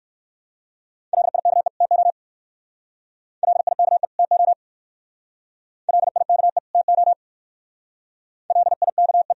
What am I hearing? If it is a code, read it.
Morse code, heard as a keyed tone, about 61 wpm, 690 Hz. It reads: LIKE TO LIKE TO LIKE TO LIKE